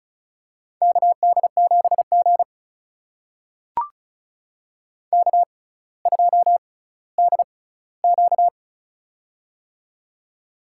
Morse code, heard as a keyed tone, about 34 wpm, spaced as for 18 wpm, 700 Hz. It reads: KD7G K2DQ
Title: USA Call Signs - 6x Character Spacing - 35wpm